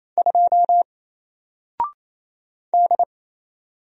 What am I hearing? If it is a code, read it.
Morse code, heard as a keyed tone, about 27 wpm, 700 Hz. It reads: 2 D
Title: Single Letter-Number Alt 28wpm